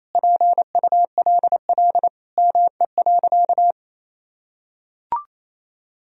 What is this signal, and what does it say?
Morse code, heard as a keyed tone, about 28 wpm, 700 Hz. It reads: PULL ME.
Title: Sentences from Top 1000 Words - Review Part 2 - 28wpm